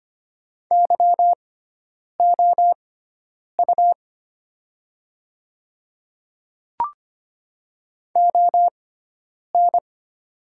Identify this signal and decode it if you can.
Morse code, heard as a keyed tone, about 25 wpm, spaced as for 9 wpm, 700 Hz. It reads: YOU ON
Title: 6x Character Spacing - Top 100 Words - 25wpm